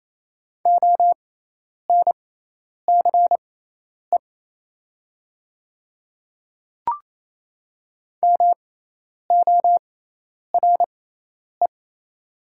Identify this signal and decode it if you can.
Morse code, heard as a keyed tone, about 28 wpm, spaced as for 10 wpm, 700 Hz. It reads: ONCE MORE
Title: 6x Character Spacing - Top 300 Words - 28wpm